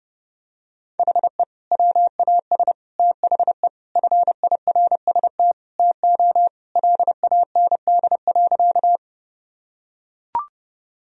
Morse code, heard as a keyed tone, about 30 wpm, 700 Hz. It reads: HE WAS THE FIRST TO LAND.
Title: Sentences from Top 1000 Words - Review Part 1 - 30wpm